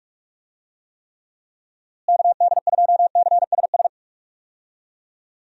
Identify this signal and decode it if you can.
Morse code, heard as a keyed tone, about 42 wpm, 700 Hz. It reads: KD2CSS
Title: USA Call Signs - 6 Characters - Repeated 2x - 45wpm